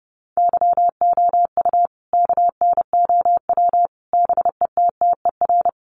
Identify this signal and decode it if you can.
Morse code, heard as a keyed tone, about 30 wpm, 700 Hz. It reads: YOU KNOW BETTER
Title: Sentences from Top 400 Words 30wpm